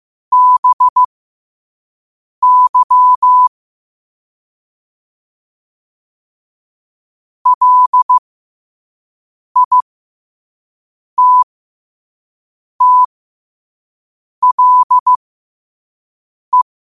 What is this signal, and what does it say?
Morse code, heard as a keyed tone, about 15 wpm, spaced as for 5 wpm, 1000 Hz. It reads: BY LITTLE
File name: qotdfiles / 5wpm